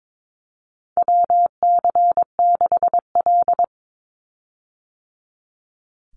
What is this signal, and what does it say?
Morse code, heard as a keyed tone, about 22 wpm, 700 Hz. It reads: WC6L
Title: USA Call Signs 22wpm